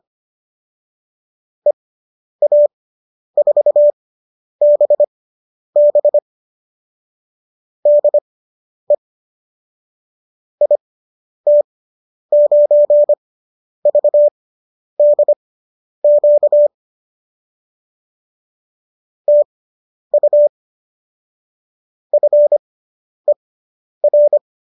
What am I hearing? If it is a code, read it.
Morse code, heard as a keyed tone, about 25 wpm, spaced as for 10 wpm, 590 Hz. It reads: EA4BB DE IT9VDQ TU FER